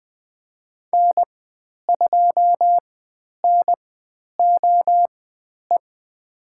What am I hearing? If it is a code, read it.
Morse code, heard as a keyed tone, about 20 wpm, spaced as for 10 wpm, 700 Hz. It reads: N2NOE
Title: 109 Lesson 18g - Letter m - Call Signs 20wpm